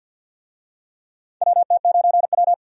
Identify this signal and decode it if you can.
Morse code, heard as a keyed tone, about 52 wpm, 700 Hz. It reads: WT9W
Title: USA Call Signs - 4 Characters - Repeated 3x - 50wpm